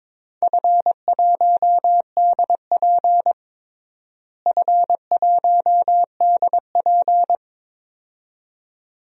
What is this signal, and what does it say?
Morse code, heard as a keyed tone, about 22 wpm, 700 Hz. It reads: F1DP F1DP
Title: DXCC-100 Call Signs - 4 Characters - Repeated 2x - 22wpm